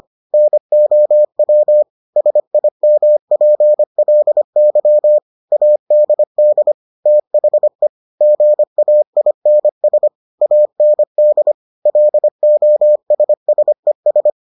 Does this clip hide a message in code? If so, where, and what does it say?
Yes: Morse code from 0.3 to 14.3 s, heard as a keyed tone, about 25 wpm, 600 Hz. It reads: NOW SIMPLY ADD THE GAINS AND LOSSES